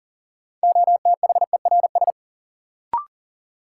Morse code, heard as a keyed tone, about 40 wpm, 700 Hz. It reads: OTHERS
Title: Sentences from Top 1500 Words - New Word Drill Alt - 40wpm